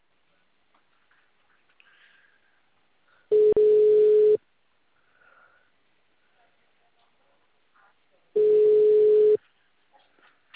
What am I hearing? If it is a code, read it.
Ringback tone (Europe, 425 Hz).
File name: out- -2006- - - .106